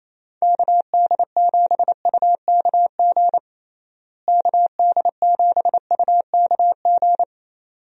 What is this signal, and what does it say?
Morse code, heard as a keyed tone, about 28 wpm, 700 Hz. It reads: KD7UKG KD7UKG
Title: USA Call Signs - 6 Characters - Repeated 2x - 28wpm